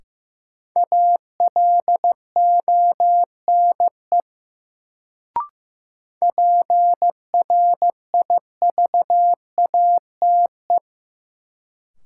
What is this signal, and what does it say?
Morse code, heard as a keyed tone, about 16 wpm, 700 Hz. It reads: ALONE PRIVATE
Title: Sentences from Top 1000 Words - New Word Drill - 15wpm